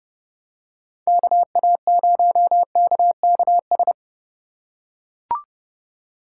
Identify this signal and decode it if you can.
Morse code, heard as a keyed tone, about 30 wpm, 700 Hz. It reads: KA0KKS
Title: USA Call Signs - 6 Characters - 8x Character Spacing - 30wpm